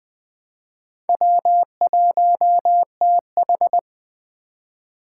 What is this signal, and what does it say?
Morse code, heard as a keyed tone, about 20 wpm, 700 Hz. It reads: W1TH
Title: Speed Racing - USA 1x2 Call Signs 17wpm